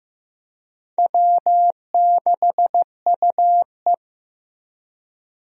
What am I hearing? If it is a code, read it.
Morse code, heard as a keyed tone, about 15 wpm, 700 Hz. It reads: W6UE
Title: USA Call Signs - 4 Characters - Repeated 3x - 15wpm